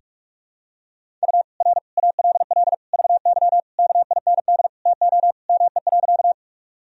Morse code, heard as a keyed tone, about 47 wpm, 700 Hz. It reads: U R ALL VY KIND TO ME.
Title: Sentences from Top 200 Words and Abbreviations 45wpm